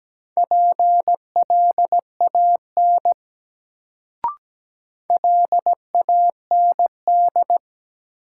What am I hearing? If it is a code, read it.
Morse code, heard as a keyed tone, about 17 wpm, 700 Hz. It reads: PLAN LAND
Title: Sentences from Top 500 Words - New Word Drill - 17wpm